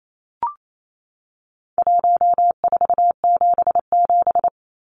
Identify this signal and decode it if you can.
Morse code, heard as a keyed tone, about 28 wpm, 700 Hz. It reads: 1477
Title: Numbers - Contest Style - 4-Digit 28wpm